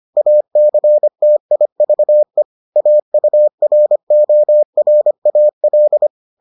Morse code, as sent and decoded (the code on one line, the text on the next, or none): .- -.-. - .. ...- . / .- ..- .-. --- .-. .- .-..
ACTIVE AURORAL